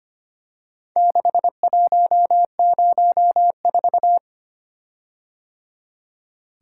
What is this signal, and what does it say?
Morse code, heard as a keyed tone, about 25 wpm, 700 Hz. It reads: 6104
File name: practice / 25wpm / Four Letter-Number Combinations - 25wpm